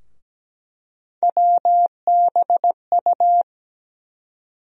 Morse code, heard as a keyed tone, about 17 wpm, 700 Hz. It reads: WBU
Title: USA Call Sign Postfixes - 3 Letters 17wpm